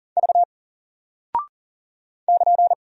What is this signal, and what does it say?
Morse code, heard as a keyed tone, about 40 wpm, 700 Hz. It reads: V <KN>
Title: Single Character (Letter:Number:Punctuation:Prosign) Alt 40wpm